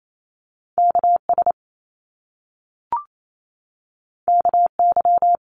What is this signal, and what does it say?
Morse code, heard as a keyed tone, about 28 wpm, 700 Hz. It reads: KS KY
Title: US States 28wpm